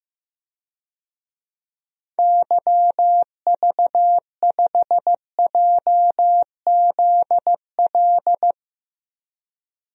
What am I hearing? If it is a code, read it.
Morse code, heard as a keyed tone, about 14 wpm, 700 Hz. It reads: YV5JZL
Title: DXCC-100 Call Signs - 6 Characters - Repeated 2x - 15wpm